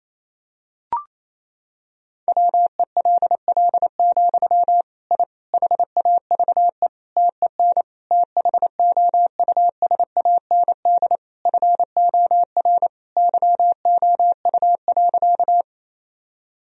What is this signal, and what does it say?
Morse code, heard as a keyed tone, about 28 wpm, 700 Hz. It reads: WELL, I HAVE TEN THOUSAND FOR YOU.
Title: Sentences from Top 1000 Words - Review Part 3 - 28wpm